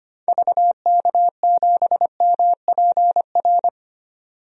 Morse code, heard as a keyed tone, about 25 wpm, 700 Hz. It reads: VK7MPR
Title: DXCC-100 Call Signs - 6 Characters - 25wpm